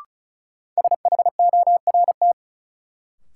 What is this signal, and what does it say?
Morse code, heard as a keyed tone, about 36 wpm, 700 Hz. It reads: SHORT
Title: Sets of 5 Words Encore - New Word Drill - 35wpm